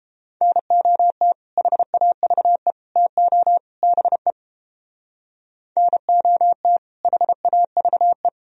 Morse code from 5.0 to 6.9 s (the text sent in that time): NOT